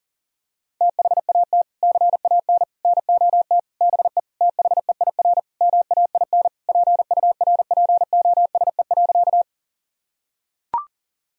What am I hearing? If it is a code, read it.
Morse code, heard as a keyed tone, about 40 wpm, 700 Hz. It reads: THAT CAN NOT BE THEIR MAIN PURPOSE.
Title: Sentences from Top 1000 Words - Review Part 2 - 40wpm